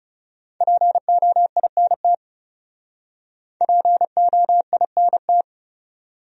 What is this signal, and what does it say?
Morse code, heard as a keyed tone, about 40 wpm, 700 Hz. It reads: POINT POINT